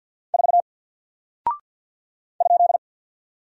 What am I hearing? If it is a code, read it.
Morse code, heard as a keyed tone, about 48 wpm, 700 Hz. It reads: 4 ?
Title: Single Character (Letter:Number:Punctuation:Prosign) Alt 50wpm